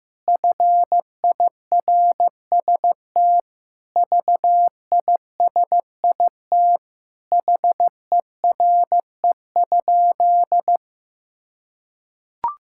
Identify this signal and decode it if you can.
Morse code, heard as a keyed tone, about 16 wpm, 700 Hz. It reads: FIRST VISIT HERE?
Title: Sentences from Top 1000 Words - Review Part 6 - 15wpm